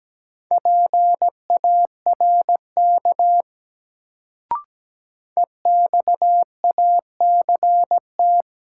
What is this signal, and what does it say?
Morse code, heard as a keyed tone, about 17 wpm, 700 Hz. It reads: PARK EXACT
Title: Sentences from Top 1300 Words - New Word Drill - 17wpm